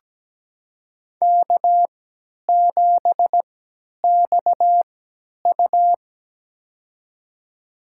Morse code, heard as a keyed tone, about 17 wpm, spaced as for 10 wpm, 700 Hz. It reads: K7XU
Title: USA Call Signs - 3x Character Spacing - 17wpm